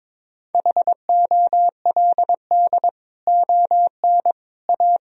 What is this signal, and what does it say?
Morse code, heard as a keyed tone, about 22 wpm, 700 Hz. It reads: HOLD ON A
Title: Sentences from Top 1000 Words - Review Part 2 - 22wpm